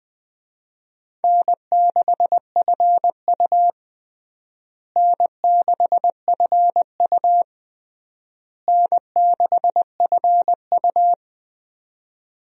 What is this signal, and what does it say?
Morse code, heard as a keyed tone, about 20 wpm, 700 Hz. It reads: N6FU N6FU N6FU
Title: USA Call Signs - 4 Characters - Repeated 3x - 20wpm